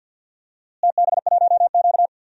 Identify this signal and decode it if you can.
Morse code, heard as a keyed tone, about 53 wpm, 700 Hz. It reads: TB1Q
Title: DXCC-100 Call Signs - 4 Characters - Repeated 2x - 50wpm